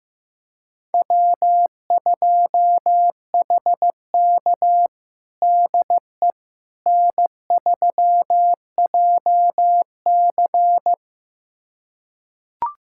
Morse code, heard as a keyed tone, about 14 wpm, 700 Hz. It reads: W2HK DE N3JC
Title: USA Callsign DE Callsign - 4 Characters - 15wpm